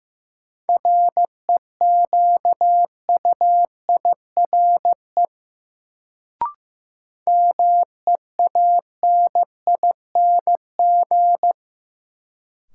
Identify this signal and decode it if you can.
Morse code, heard as a keyed tone, about 15 wpm, 700 Hz. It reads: REQUIRE MEANING
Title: Sentences from Top 900 Words - New Word Drill - 15wpm